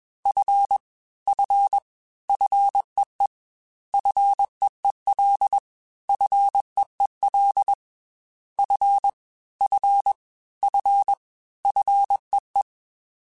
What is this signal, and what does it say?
Morse code, heard as a keyed tone, about 21 wpm, 800 Hz. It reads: F F FEE FEEL FEEL F F F FEE
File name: F and L / F and L mp3 / F and L Prefixes 3x2 Twice @20